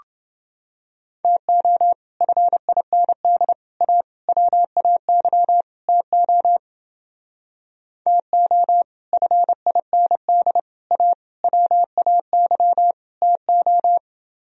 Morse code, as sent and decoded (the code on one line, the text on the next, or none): - --- / ..-. .. -. -.. / .- / .-- .- -.-- / - --- / - --- / ..-. .. -. -.. / .- / .-- .- -.-- / - ---
TO FIND A WAY TO TO FIND A WAY TO